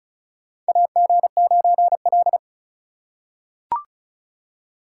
Morse code, heard as a keyed tone, about 36 wpm, 700 Hz. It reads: AG9L
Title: USA Call Signs - 4 Characters - 2x Character Spacing - 35wpm